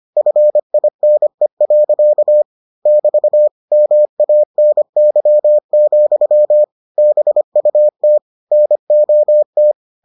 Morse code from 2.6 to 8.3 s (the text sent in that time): <BT> MANY, BUT